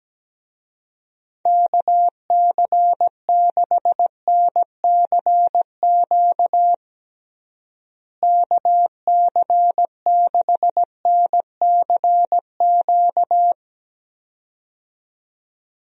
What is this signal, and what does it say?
Morse code, heard as a keyed tone, about 17 wpm, 700 Hz. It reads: KC6NCQ KC6NCQ
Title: USA Call Signs - 6 Characters - Repeated 2x - 17wpm